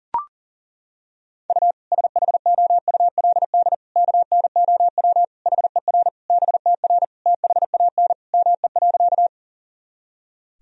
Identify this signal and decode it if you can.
Morse code, heard as a keyed tone, about 37 wpm, 700 Hz. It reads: U SHOULD KNOW HER BTR THAN ME.
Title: Sentences from Top 400 Words and Abbreviations 40wpm